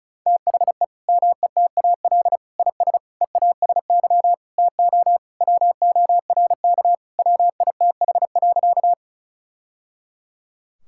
Morse code, heard as a keyed tone, about 36 wpm, 700 Hz. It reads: THE METAL IS EASY TO WORK WITH.
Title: Sentences from Top 1000 Words - Review Part 6 - 35wpm